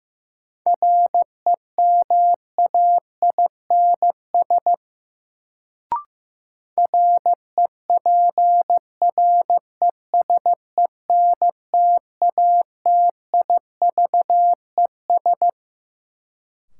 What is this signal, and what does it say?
Morse code, heard as a keyed tone, about 15 wpm, 700 Hz. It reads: REMAINS REPRESENTATIVES
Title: Sets of 5 Words Encore - New Word Drill - 15wpm